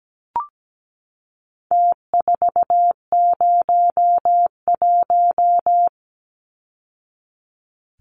Morse code, heard as a keyed tone, about 17 wpm, 700 Hz. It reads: T401
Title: Numbers - Contest Style - 4-Digit 17wpm